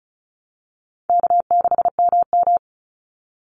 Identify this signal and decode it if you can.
Morse code, heard as a keyed tone, about 34 wpm, 700 Hz. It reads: K6MM
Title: Call Signs Repeated 2x - 35wpm